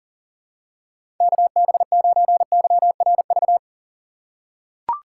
Morse code, heard as a keyed tone, about 40 wpm, 700 Hz. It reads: KB9YRV
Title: USA Call Signs - 6 Characters - 4x Character Spacing - 40wpm